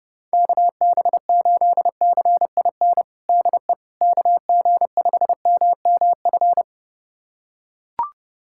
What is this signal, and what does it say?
Morse code, heard as a keyed tone, about 30 wpm, 700 Hz. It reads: KB8CIN DE KG5MMF
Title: USA Callsign DE Callsign - 6 Characters - 30wpm